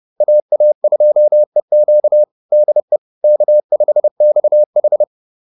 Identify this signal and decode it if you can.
Morse code, heard as a keyed tone, about 30 wpm, 600 Hz. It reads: AA2EQ DE K5XH